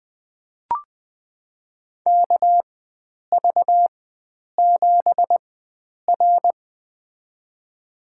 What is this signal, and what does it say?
Morse code, heard as a keyed tone, about 20 wpm, spaced as for 9 wpm, 700 Hz. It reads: KV7R
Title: USA Call Signs - 4x Character Spacing - 20wpm